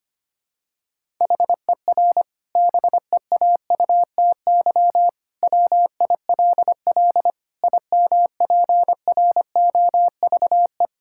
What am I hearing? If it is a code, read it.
Morse code, heard as a keyed tone, about 25 wpm, 700 Hz. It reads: HER BEAUTY WILL IMPROVE
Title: Sentences from Top 1000 Words - Review Part 5 - 25wpm